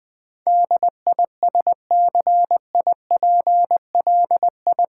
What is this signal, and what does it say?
Morse code, heard as a keyed tone, about 20 wpm, 700 Hz. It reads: DISCIPLI
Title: Build A Word - 4 Letters At A Time - 10-Letter Words - 20wpm